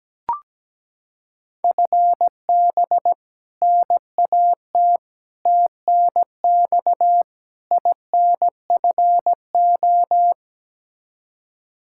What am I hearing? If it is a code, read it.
Morse code, heard as a keyed tone, about 17 wpm, 700 Hz. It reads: FB NAT TNX INFO
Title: Practice QSO - Exchange 3 - More Info - 17wpm